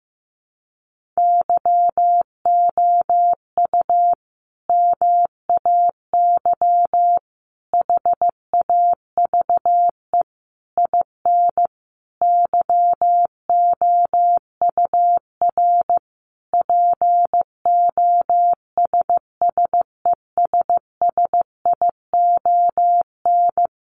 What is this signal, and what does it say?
Morse code, heard as a keyed tone, about 15 wpm, 700 Hz. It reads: YOU MAY HAVE IN YOUR POSSESSION
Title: Sentences from Top 1400 Words 15wpm